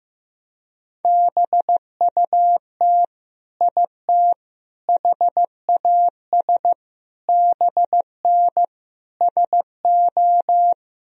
Morse code, heard as a keyed tone, about 15 wpm, 700 Hz. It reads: BUT IT HAS BN SO
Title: Sentences from Top 100 Words and Abbreviations 15wpm